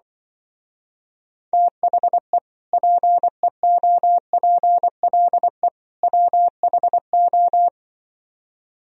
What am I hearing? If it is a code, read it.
Morse code, heard as a keyed tone, about 24 wpm, 700 Hz. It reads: THE PEOPLE WHO
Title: Speed-Racing - Sets of 3 Words 20wpm